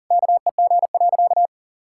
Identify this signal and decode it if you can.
Morse code, heard as a keyed tone, about 40 wpm, 700 Hz. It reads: KEG.
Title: Pangram Challenge v2 - 40wpm